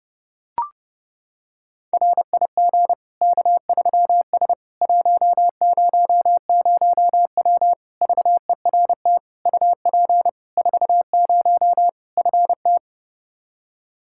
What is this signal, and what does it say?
Morse code, heard as a keyed tone, about 30 wpm, 700 Hz. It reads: RIG K3S 100W VERT UP 40 FT
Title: K1USN SST QSO Practice 30wpm